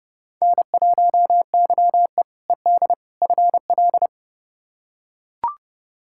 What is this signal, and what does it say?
Morse code, heard as a keyed tone, about 30 wpm, 700 Hz. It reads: N1YE ED FL
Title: K1USN SST QSO Practice - 30wpm